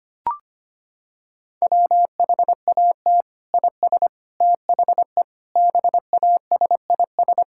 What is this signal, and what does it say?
Morse code, heard as a keyed tone, about 25 wpm, 700 Hz. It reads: WHAT IS THE BASIS
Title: Sentences from Top 1000 Words - Review Part 6 - 25wpm